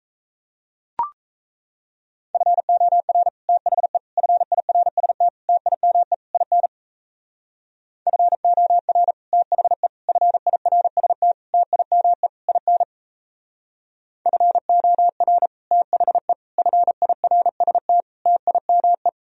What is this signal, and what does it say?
Morse code, heard as a keyed tone, about 37 wpm, 700 Hz. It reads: FOR THE FIRST TIME IN FOR THE FIRST TIME IN FOR THE FIRST TIME